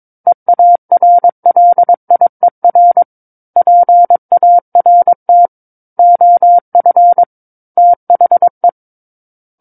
Morse code, heard as a keyed tone, about 22 wpm, 700 Hz. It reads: EARLIER PART OF THE
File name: Book One / 13-Chapter 13 - 22 WPM